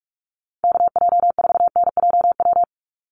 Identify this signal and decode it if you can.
Morse code, heard as a keyed tone, about 44 wpm, 700 Hz. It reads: KJ4NJW
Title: USA Call Signs 45wpm